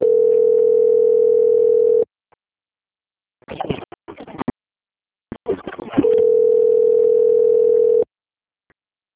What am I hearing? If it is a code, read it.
Ringback tone (North America), 440 + 480 Hz.